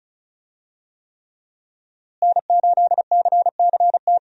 Morse code, heard as a keyed tone, about 36 wpm, 700 Hz. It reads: N8CCT
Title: USA Call Signs - 5 Characters - Repeated 3x - 35wpm